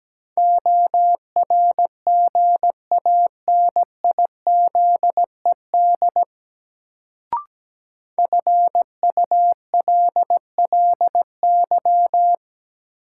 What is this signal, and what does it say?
Morse code, heard as a keyed tone, about 17 wpm, 700 Hz. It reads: ORGANIZED FULLY
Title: Sentences from Top 1300 Words - New Word Drill - 17wpm